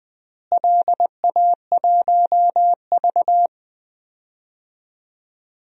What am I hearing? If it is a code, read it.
Morse code, heard as a keyed tone, about 20 wpm, 700 Hz. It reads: LA1V
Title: DXCC-100 Call Signs - 4 Characters - Repeated 2x - 20wpm